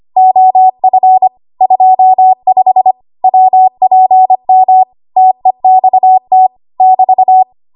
Morse code, heard as a keyed tone, about 25 wpm, 750 Hz. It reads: OF 25 WPM TEXT <BT>
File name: Morse / 25WPM